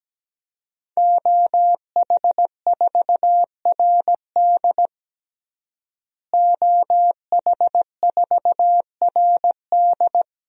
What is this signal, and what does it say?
Morse code, heard as a keyed tone, about 17 wpm, 700 Hz. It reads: OH4RD OH4RD
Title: DXCC-100 Call Signs - 5 Characters - Repeated 3x - 17wpm